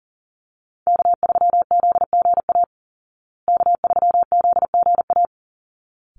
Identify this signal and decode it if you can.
Morse code, heard as a keyed tone, about 40 wpm, 700 Hz. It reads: K3ZGA K3ZGA
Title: Call Signs Repeated 3x - 40wpm